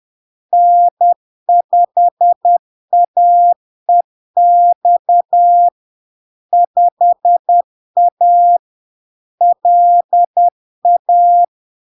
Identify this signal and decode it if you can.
Morse code, heard as a keyed tone, about 10 wpm, 700 Hz. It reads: N5AEX 5A LA
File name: LARK-FD-2022-Log-in-CW-10wpm-700-Hz